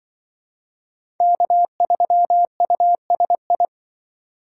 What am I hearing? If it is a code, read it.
Morse code, heard as a keyed tone, about 25 wpm, 700 Hz. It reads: K3USI
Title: Speed Racing - USA 1x3 Call Signs 20wpm